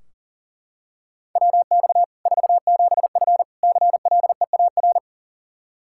Morse code, heard as a keyed tone, about 40 wpm, 700 Hz. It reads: WX 47F CLEAR
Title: Practice QSO - Exchange 2 - Info - 40wpm